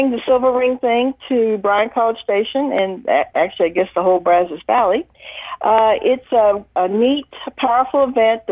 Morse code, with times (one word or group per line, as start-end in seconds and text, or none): none